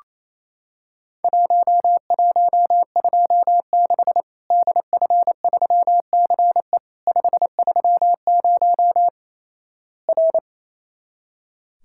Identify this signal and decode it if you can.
Morse code, heard as a keyed tone, about 28 wpm, 700 Hz. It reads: 1126 DF3CE 530 R
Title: Worked All Europe DX Contest - QTC Exchange - 28wpm